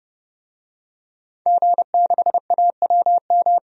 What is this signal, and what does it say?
Morse code, heard as a keyed tone, about 30 wpm, 700 Hz. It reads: G6AWM
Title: DXCC-100 Call Signs - 5 Characters - 30wpm